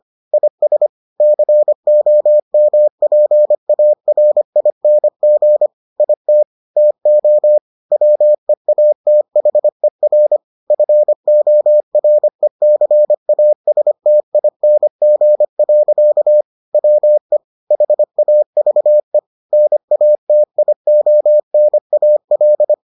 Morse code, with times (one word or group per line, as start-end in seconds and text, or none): 0.3-0.9 IS
1.2-5.7 COMPARING
6.0-6.4 IT
6.8-7.6 TO
7.9-10.4 WEATHER
10.7-16.4 FORECASTING.
16.7-17.4 WE
17.7-19.2 HAVE
19.5-22.7 NATIONAL